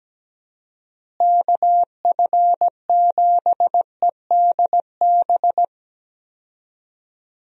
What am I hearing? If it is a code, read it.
Morse code, heard as a keyed tone, about 17 wpm, 700 Hz. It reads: KF7EDB
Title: USA Call Signs - 6 Characters - Repeated 2x - 17wpm